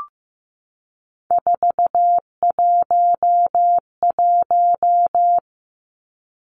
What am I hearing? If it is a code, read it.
Morse code, heard as a keyed tone, about 15 wpm, 700 Hz. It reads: 411
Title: Numbers - 3-Digit 15wpm